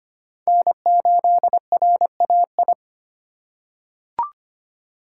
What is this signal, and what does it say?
Morse code, heard as a keyed tone, about 25 wpm, 700 Hz. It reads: N8RAI
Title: USA Call Signs - 5 Characters - 2x Character Spacing - 25wpm